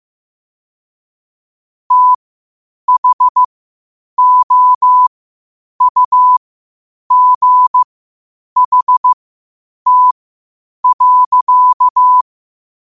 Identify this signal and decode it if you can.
Morse code, heard as a keyed tone, about 15 wpm, spaced as for 8 wpm, 1000 Hz. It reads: THOUGHT.
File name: qotdfiles / 8wpm